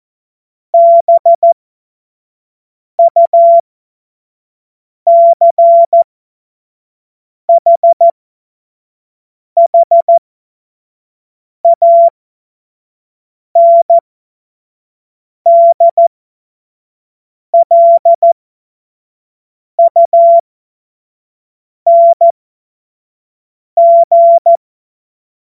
Morse code, heard as a keyed tone, about 14 wpm, spaced as for 5 wpm, 680 Hz. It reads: BUCHHANDLUNG
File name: Morsekurs / Witze / Witz 30 Januar 2024 25 F70